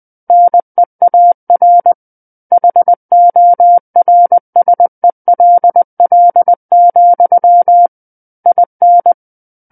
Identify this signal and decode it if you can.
Morse code, heard as a keyed tone, about 20 wpm, 700 Hz. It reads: NEAR HORSELL, IN
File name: 20 WPM